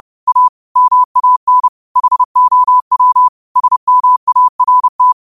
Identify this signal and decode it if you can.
Morse code, heard as a keyed tone, about 30 wpm, 1000 Hz. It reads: A MAN HOW SMART
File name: qotdfiles / 30wpm